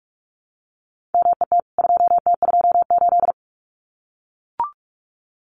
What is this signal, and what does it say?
Morse code, heard as a keyed tone, about 44 wpm, 700 Hz. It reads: MET 2T28
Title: CWT Exchange 45wpm